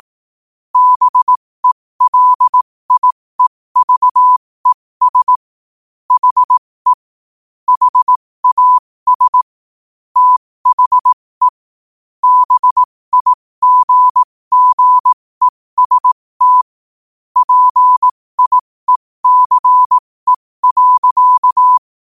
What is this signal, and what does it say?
Morse code, heard as a keyed tone, about 18 wpm, spaced as for 15 wpm, 1000 Hz. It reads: BELIEVES HE HAS THE BIGGEST PIECE.